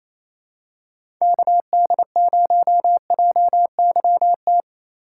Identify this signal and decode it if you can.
Morse code, heard as a keyed tone, about 28 wpm, 700 Hz. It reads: KD0JYT
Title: USA Call Signs - 6 Characters - 2x Character Spacing - 28wpm